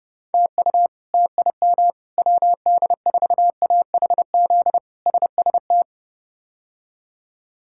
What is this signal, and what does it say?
Morse code, heard as a keyed tone, about 30 wpm, 700 Hz. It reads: TU TIM WD4AHZ SST
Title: K1USN SST QSO Practice - 30wpm